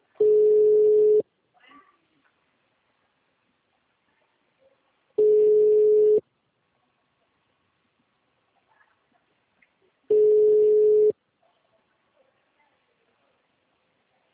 Ringback tone (Europe, 425 Hz).